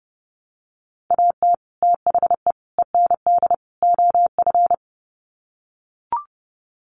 Morse code, heard as a keyed tone, about 30 wpm, 700 Hz. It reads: AT THE END OF
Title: Sets of 4 Words 30wpm